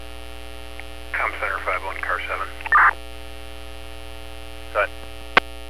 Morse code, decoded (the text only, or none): none